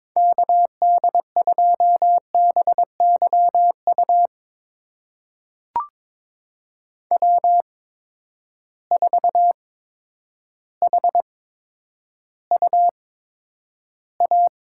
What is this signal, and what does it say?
Morse code, heard as a keyed tone, about 22 wpm, spaced as for 10 wpm, 700 Hz. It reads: KD2BYU W4HUA
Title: USA Call Signs - 8x Character Spacing - 22wpm